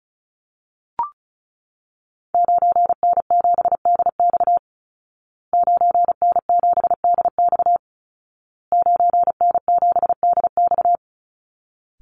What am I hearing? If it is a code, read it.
Morse code, heard as a keyed tone, about 34 wpm, 700 Hz. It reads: 9N7DX 9N7DX 9N7DX
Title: Call Signs Repeated 2x - 35wpm